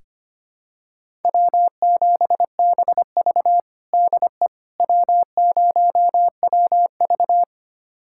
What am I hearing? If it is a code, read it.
Morse code, heard as a keyed tone, about 25 wpm, 700 Hz. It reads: W7BV DE W0WV
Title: USA Callsign DE Callsign - 4 Characters - 25wpm